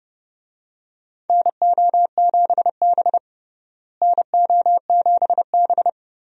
Morse code, heard as a keyed tone, about 32 wpm, 700 Hz. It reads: NO7B NO7B
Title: USA Call Signs - 4 Characters - Repeated 2x - 30wpm